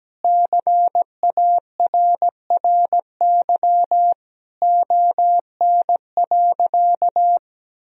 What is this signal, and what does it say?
Morse code, heard as a keyed tone, about 17 wpm, 700 Hz. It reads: CARRY ON.
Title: Sentences from Top 1000 Words - Review Part 1 - 17wpm